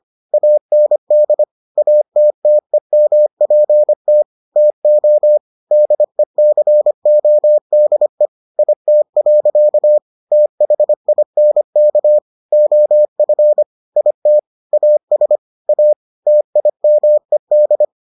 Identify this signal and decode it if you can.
Morse code, heard as a keyed tone, about 25 wpm, 600 Hz. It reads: AND ATTEMPT TO DECODE IT. THINK OF IT AS A TIMED